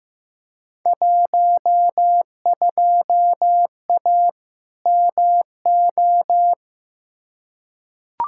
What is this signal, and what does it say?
Morse code, heard as a keyed tone, about 15 wpm, 700 Hz. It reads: 12A MO
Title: Field Day Exchanges Encore - 15wpm